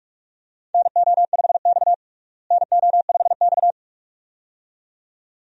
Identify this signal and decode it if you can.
Morse code, heard as a keyed tone, about 42 wpm, 700 Hz. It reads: NO5X NO5X
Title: USA Call Signs - 4 Characters - Repeated 2x - 45wpm